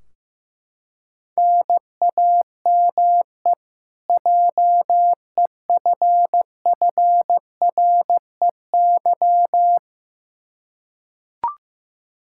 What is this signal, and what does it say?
Morse code, heard as a keyed tone, about 15 wpm, 700 Hz. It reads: NAME JEFFREY
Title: Practice QSO - Exchange 1 - Beginning - 15wpm